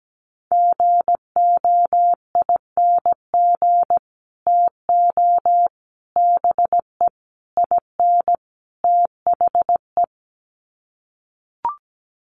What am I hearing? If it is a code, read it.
Morse code, heard as a keyed tone, about 17 wpm, 700 Hz. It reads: GOING TO BE IN THE
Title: Sets of 5 Words Encore 17wpm